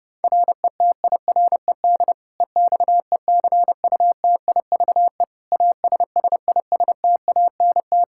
Morse code, heard as a keyed tone, about 30 wpm, 700 Hz. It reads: RETIRED EXECUTIVE ASSISTANT
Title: K1USN SST QSO Practice 30wpm